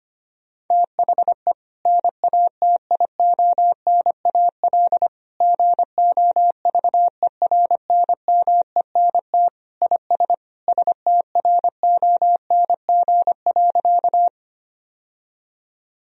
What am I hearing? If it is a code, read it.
Morse code, heard as a keyed tone, about 25 wpm, 700 Hz. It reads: THE NATIONAL GOVERNMENT IS STRONG.
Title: Sentences from Top 1000 Words - Review Part 4 - 25wpm